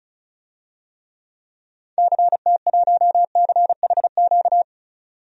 Morse code, heard as a keyed tone, about 36 wpm, 700 Hz. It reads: CT1CHQ
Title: DXCC-100 Call Signs - 6 Characters - 35wpm